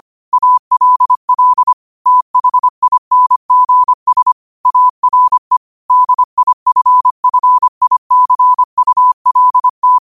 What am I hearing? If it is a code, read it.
Morse code, heard as a keyed tone, about 25 wpm, 1000 Hz. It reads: ALL THINGS ARE DIFFICULT
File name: qotdfiles / 25wpm